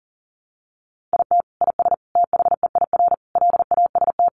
Morse code, heard as a keyed tone, about 40 wpm, 700 Hz. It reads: IT IS THEIR LAST
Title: Sentences from Top 1300 Words 40wpm